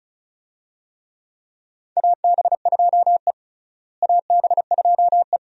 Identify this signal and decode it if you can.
Morse code, heard as a keyed tone, about 36 wpm, 700 Hz. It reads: AB2E AB2E
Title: USA Call Signs - 4 Characters - Repeated 2x - 35wpm